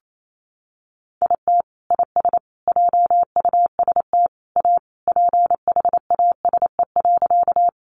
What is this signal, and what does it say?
Morse code, heard as a keyed tone, about 28 wpm, 700 Hz. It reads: IT IS JUST A PHASE.
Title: Sentences from Top 1200 Words 28wpm